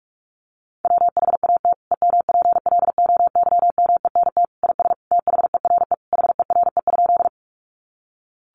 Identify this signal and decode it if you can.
Morse code, heard as a keyed tone, about 44 wpm, 700 Hz. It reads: WHAT EMPLOYMENT IS THERE HERE?
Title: Sentences from Top 1500 Words 45wpm